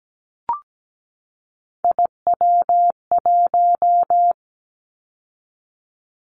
Morse code, heard as a keyed tone, about 17 wpm, 700 Hz. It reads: IW1
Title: Callsign Prefixes for Top 25 DXCC - 17wpm